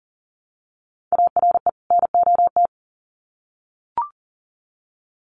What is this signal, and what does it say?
Morse code, heard as a keyed tone, about 40 wpm, 700 Hz. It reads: ARE NOT
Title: Sets of 2 Words 40wpm